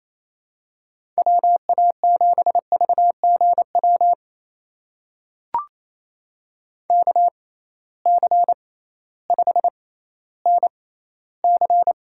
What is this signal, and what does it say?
Morse code, heard as a keyed tone, about 28 wpm, spaced as for 15 wpm, 700 Hz. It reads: WA7VGW KC5NC
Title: USA Call Signs - 6 Characters - 6x Character Spacing - 28wpm